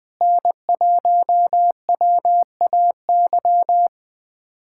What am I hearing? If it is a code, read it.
Morse code, heard as a keyed tone, about 20 wpm, 700 Hz. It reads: N1WAY
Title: USA Callsign DE Callsign 20wpm